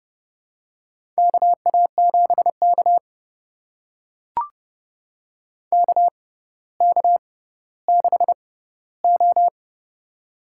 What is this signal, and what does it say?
Morse code, heard as a keyed tone, about 30 wpm, spaced as for 15 wpm, 700 Hz. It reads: KA7K KK6O
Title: USA Call Signs - 4 Characters - 6x Character Spacing - 30wpm